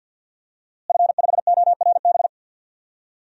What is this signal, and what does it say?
Morse code, heard as a keyed tone, about 46 wpm, 700 Hz. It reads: F5ORB
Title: DXCC-100 Call Signs - 5 Characters - Repeated 2x - 50wpm